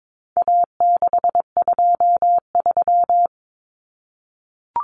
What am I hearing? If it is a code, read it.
Morse code, heard as a keyed tone, about 22 wpm, 700 Hz. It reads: A623
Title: Common Cut Numbers - 4-Digit 22wpm